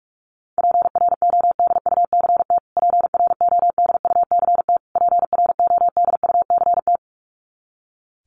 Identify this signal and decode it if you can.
Morse code, heard as a keyed tone, about 44 wpm, 700 Hz. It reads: PRODUCT PRODUCT PRODUCT
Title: Top 500 Words Repeated 45wpm